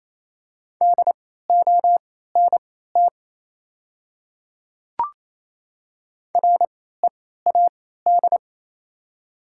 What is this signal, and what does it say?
Morse code, heard as a keyed tone, about 28 wpm, spaced as for 16 wpm, 700 Hz. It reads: DONT READ
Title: 3x Character Spacing - Top 300 Words - 28wpm